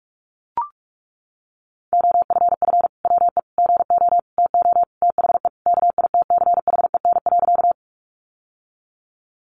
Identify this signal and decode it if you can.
Morse code, heard as a keyed tone, about 44 wpm, 700 Hz. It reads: OFF WE GO TO THE KITCHEN.
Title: Sentences from Top 1400 Words 45wpm